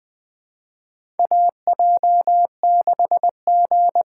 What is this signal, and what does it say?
Morse code, heard as a keyed tone, about 20 wpm, 700 Hz. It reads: AJ6G
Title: USA Call Signs - 5 Characters - 6x Character Spacing - 20wpm